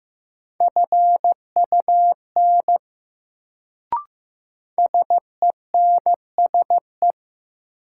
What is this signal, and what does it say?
Morse code, heard as a keyed tone, about 15 wpm, 700 Hz. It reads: FUN SENSE
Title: Sentences from Top 700 Words - New Word Drill - 15wpm